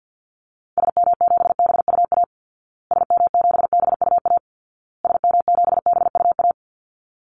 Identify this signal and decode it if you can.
Morse code, heard as a keyed tone, about 50 wpm, 700 Hz. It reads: SM7BUA SM7BUA SM7BUA
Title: Call Signs Repeated 2x - 50wpm